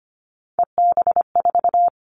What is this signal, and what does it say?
Morse code, heard as a keyed tone, about 25 wpm, 700 Hz. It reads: EB4
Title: All Cut Numbers - 3-Digit 25wpm